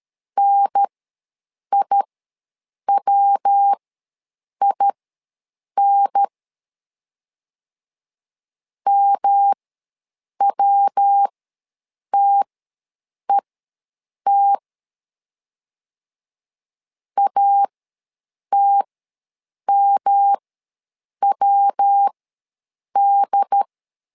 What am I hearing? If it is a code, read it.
Morse code, heard as a keyed tone, about 13 wpm, spaced as for 7 wpm, 800 Hz. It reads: NIWIN MWTET ATMWD
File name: LEZIONI / 3 MWRO / 3 MW ws 12wpm 1816